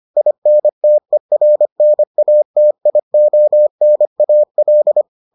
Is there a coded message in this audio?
Yes: Morse code from 0.2 to 5.0 s, heard as a keyed tone, about 25 wpm, 600 Hz. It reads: INTERNATIONAL